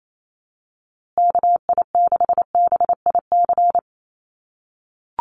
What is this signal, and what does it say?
Morse code, heard as a keyed tone, about 28 wpm, 700 Hz. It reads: KI6BIC
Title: USA Call Signs Repeated 3x 28wpm